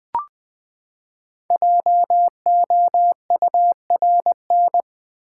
Morse code, heard as a keyed tone, about 20 wpm, 700 Hz. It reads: JOURN
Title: Build A Word - 5 Letters At A Time - 10-Letter Words - 20wpm